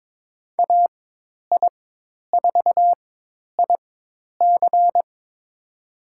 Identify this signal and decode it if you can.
Morse code, heard as a keyed tone, about 22 wpm, spaced as for 10 wpm, 700 Hz. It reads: AI4IC
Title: USA Call Signs - 5 Characters - 4x Character Spacing - 22wpm